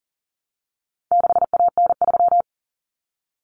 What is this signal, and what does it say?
Morse code, heard as a keyed tone, about 40 wpm, 700 Hz. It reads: 6AN3
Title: Common Cut Numbers - 4-Digit 40wpm